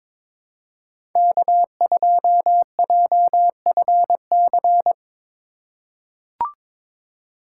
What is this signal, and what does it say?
Morse code, heard as a keyed tone, about 22 wpm, 700 Hz. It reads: K2JFC
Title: USA Call Signs - 4x Character Spacing - 22wpm